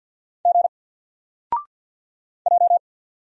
Morse code, heard as a keyed tone, about 50 wpm, 700 Hz. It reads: G J
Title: Single Letter-Number 50wpm